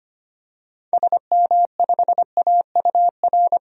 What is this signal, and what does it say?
Morse code, heard as a keyed tone, about 25 wpm, 700 Hz. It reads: SM5AUR
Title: DXCC-100 Call Signs - 6 Characters - 25wpm